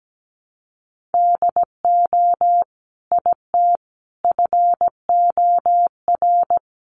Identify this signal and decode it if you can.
Morse code, heard as a keyed tone, about 17 wpm, 700 Hz. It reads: DO IT FOR
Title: Sentences from Top 200 Words 17wpm